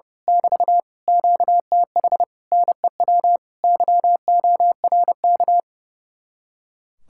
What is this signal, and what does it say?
Morse code, heard as a keyed tone, about 28 wpm, 700 Hz. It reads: <BT> QTH NEW YORK
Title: Speed-Racing Common QSO Phrases Encore - 30wpm